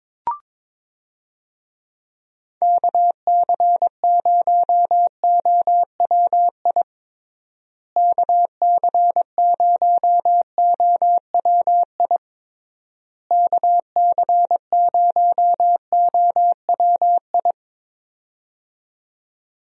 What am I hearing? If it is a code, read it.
Morse code, heard as a keyed tone, about 22 wpm, 700 Hz. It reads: KC0OWI KC0OWI KC0OWI
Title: USA Call Signs - 6 Characters - Repeated 3x - 22wpm